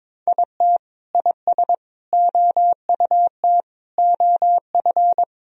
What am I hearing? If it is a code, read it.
Morse code, heard as a keyed tone, about 21 wpm, 700 Hz. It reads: IT IS OUT OF
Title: Sentences from Top 1000 Words - Review Part 2 - 22wpm